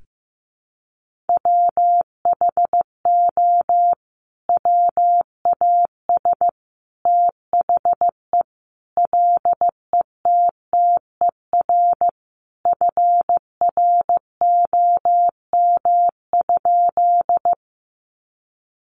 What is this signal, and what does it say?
Morse code, heard as a keyed tone, about 15 wpm, 700 Hz. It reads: WHO WAS THE LETTER FROM?
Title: Sentences from Top 300 Words 15wpm